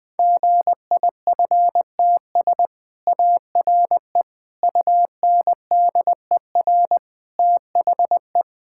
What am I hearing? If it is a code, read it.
Morse code, heard as a keyed tone, about 20 wpm, 700 Hz. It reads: GIFTS ARE UNDER THE
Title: Easy Sentences with Hints - 20wpm